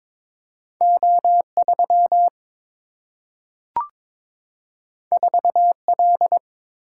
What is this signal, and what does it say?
Morse code, heard as a keyed tone, about 22 wpm, 700 Hz. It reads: O 3 4 L
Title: Single-Letter Single-Number 22wpm